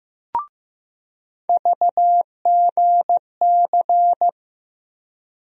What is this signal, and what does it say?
Morse code, heard as a keyed tone, about 15 wpm, 700 Hz. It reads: VGC
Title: USA Call Sign Postfixes - 3 Letters 15wpm